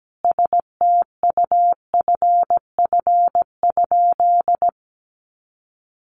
Morse code, heard as a keyed tone, about 17 wpm, 700 Hz. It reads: STUFF?
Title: Sentences from Top 1400 Words 17wpm